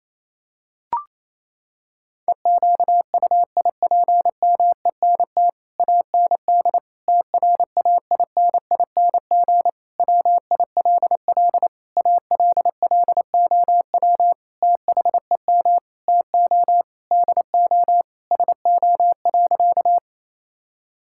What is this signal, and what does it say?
Morse code, heard as a keyed tone, about 28 wpm, 700 Hz. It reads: EQUIPMENT AND TRAINING WILL ALLOW THEM TO DO SO.
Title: Sentences from Top 1000 Words - Review Part 5 - 28wpm